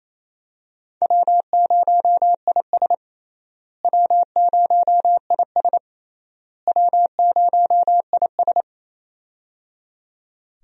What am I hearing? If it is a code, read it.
Morse code, heard as a keyed tone, about 28 wpm, 700 Hz. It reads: W0IS W0IS W0IS
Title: USA Call Signs - 4 Characters - Repeated 3x - 28wpm